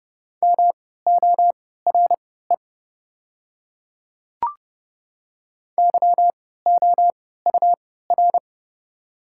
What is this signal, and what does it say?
Morse code, heard as a keyed tone, about 30 wpm, spaced as for 17 wpm, 700 Hz. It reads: MORE YOUR
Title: 3x Character Spacing - Top 100 Words - 30wpm